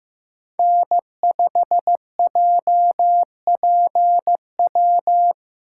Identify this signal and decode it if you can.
Morse code, heard as a keyed tone, about 15 wpm, 700 Hz. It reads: N5JPW
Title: USA Call Signs - 5 Characters - 3x Character Spacing - 15wpm